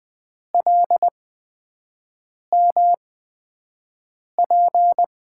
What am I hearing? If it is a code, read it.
Morse code, heard as a keyed tone, about 20 wpm, spaced as for 5 wpm, 700 Hz. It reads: LMP
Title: USA Call Signs - 8x Character Spacing - 20wpm